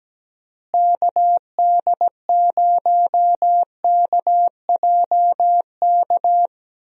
Morse code, heard as a keyed tone, about 17 wpm, 700 Hz. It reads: KD0KJK
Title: USA Call Signs - 6 Characters - Repeated 3x - 17wpm